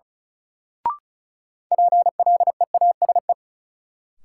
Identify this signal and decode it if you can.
Morse code, heard as a keyed tone, about 34 wpm, 700 Hz. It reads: PLEASE
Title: Sentences from Top 1500 Words - New Word Drill - 35wpm